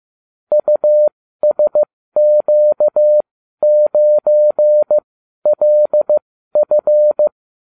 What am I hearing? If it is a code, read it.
Morse code, heard as a keyed tone, about 15 wpm, spaced as for 12 wpm, 600 Hz. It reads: USQ9LF